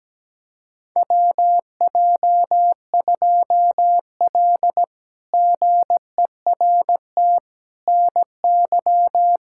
Morse code, heard as a keyed tone, about 17 wpm, 700 Hz. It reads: WJ2L GERT NY
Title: K1USN SST QSO Practice - 17wpm